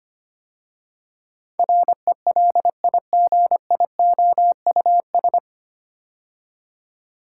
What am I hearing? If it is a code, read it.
Morse code, heard as a keyed tone, about 25 wpm, 700 Hz. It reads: RELIGIOUS
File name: practice / 25wpm / 9 Letter Words - First 3 Letters Given - 25wpm